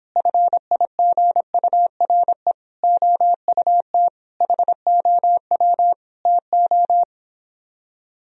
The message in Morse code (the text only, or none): FIGURE OUT HOW TO